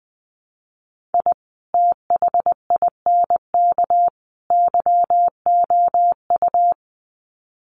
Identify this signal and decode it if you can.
Morse code, heard as a keyed tone, about 20 wpm, 700 Hz. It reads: I THINK YOU
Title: Sets of 3 Words 20wpm